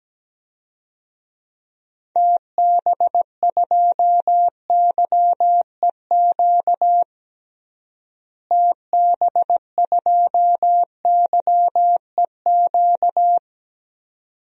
Morse code, heard as a keyed tone, about 17 wpm, 700 Hz. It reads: TB2YEQ TB2YEQ
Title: DXCC-100 Call Signs - 6 Characters - Repeated 2x - 17wpm